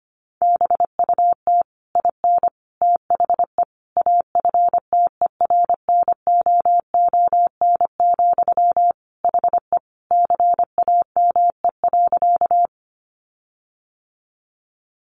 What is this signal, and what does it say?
Morse code, heard as a keyed tone, about 25 wpm, 700 Hz. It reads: BUT IN THE AFTERNOON, HE CAME.
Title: Sentences from Top 1500 Words 25wpm